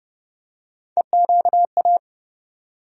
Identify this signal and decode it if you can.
Morse code, heard as a keyed tone, about 31 wpm, 700 Hz. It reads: EQA
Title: USA Call Sign Postfixes - 3 Letters 30wpm